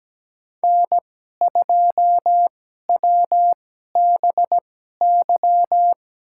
Morse code, heard as a keyed tone, about 17 wpm, spaced as for 12 wpm, 700 Hz. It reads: N2WBY
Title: USA Call Signs - 5 Characters - 2x Character Spacing - 17wpm